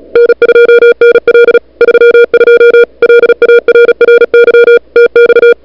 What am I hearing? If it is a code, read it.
Morse code, heard as a keyed tone, about 36 wpm, 440 Hz, with noise. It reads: N2NL 32 LARRY TX
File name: n6tr / Fab2007CWSprintHour1